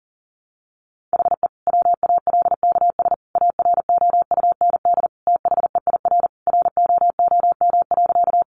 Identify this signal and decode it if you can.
Morse code, heard as a keyed tone, about 40 wpm, 700 Hz. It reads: HE WALKS AROUND THEIR ROOM.
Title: Easy Phrases 40wpm